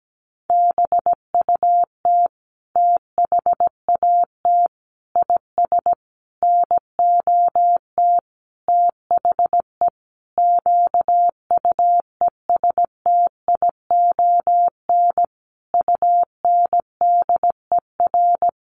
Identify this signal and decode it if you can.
Morse code, heard as a keyed tone, about 17 wpm, 700 Hz. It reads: BUT THAT IS NOT THE QUESTION UNDER
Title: Sentences from Top 1500 Words 17wpm